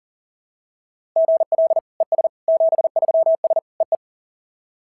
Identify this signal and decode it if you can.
Morse code, heard as a keyed tone, about 40 wpm, 690 Hz. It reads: GL ES 73S EE
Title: SOTA QSO Practice 40wpm